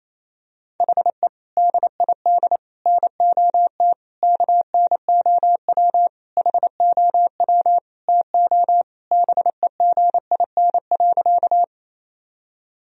Morse code, heard as a keyed tone, about 28 wpm, 700 Hz. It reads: HE DID NOT KNOW HOW TO BEGIN.
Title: Sentences from Top 1000 Words - Review Part 1 - 28wpm